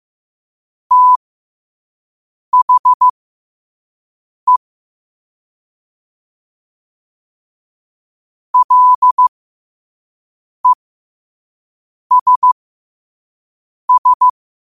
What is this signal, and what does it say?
Morse code, heard as a keyed tone, about 15 wpm, spaced as for 5 wpm, 1000 Hz. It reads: THE LESS
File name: qotdfiles / 5wpm